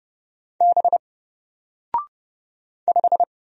Morse code, heard as a keyed tone, about 30 wpm, 700 Hz. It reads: B 5
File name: Single-Letter-Number-30wpm